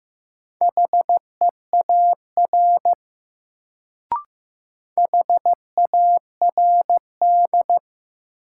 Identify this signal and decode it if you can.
Morse code, heard as a keyed tone, about 15 wpm, 700 Hz. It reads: HEAR HARD
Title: Sentences from Top 300 Words - New Word Drill - 15wpm